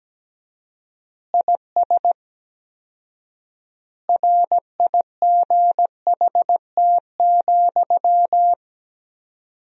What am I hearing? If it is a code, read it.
Morse code, heard as a keyed tone, about 17 wpm, 700 Hz. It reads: IS RIGHT,
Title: 4x Word-Spacing - Sentences from Top 200 Words - 17wpm